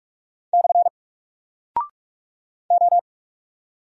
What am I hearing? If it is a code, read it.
Morse code, heard as a keyed tone, about 44 wpm, 700 Hz. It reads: / O
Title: Single Character (Letter:Number:Punctuation:Prosign) 45wpm